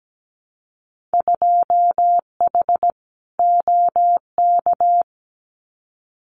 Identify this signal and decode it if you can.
Morse code, heard as a keyed tone, about 17 wpm, 700 Hz. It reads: 2H OK
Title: Winter Field Day Exchanges Encore 17wpm